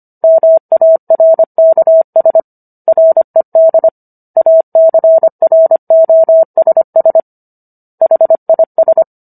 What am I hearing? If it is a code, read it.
Morse code, heard as a keyed tone, about 25 wpm, 650 Hz. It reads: MARKS RED ACROSS HIS